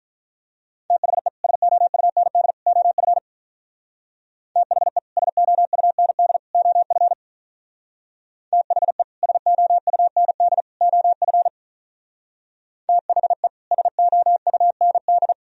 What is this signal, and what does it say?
Morse code, heard as a keyed tone, about 43 wpm, 700 Hz. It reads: THE SOUND OF THE SOUND OF THE SOUND OF THE SOUND
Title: Speed-Racing - Sets of 3 Words 35wpm